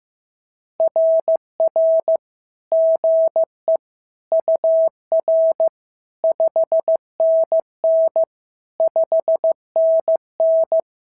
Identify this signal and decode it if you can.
Morse code, heard as a keyed tone, about 15 wpm, 690 Hz. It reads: RR GE UR 5NN 5NN
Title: SOTA QSO Practice 15wpm